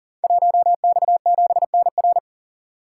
Morse code, heard as a keyed tone, about 39 wpm, 700 Hz. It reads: 1X7NR
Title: Code Groups - 5 Letter:Numbers - 40wpm